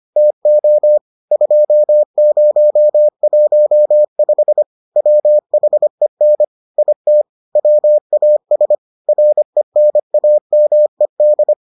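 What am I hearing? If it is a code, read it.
Morse code, heard as a keyed tone, about 25 wpm, 600 Hz. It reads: TO 2015 WHEN IT WAS RENAMED
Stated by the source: Morse edition of the Foundations of Amateur Radio episodes.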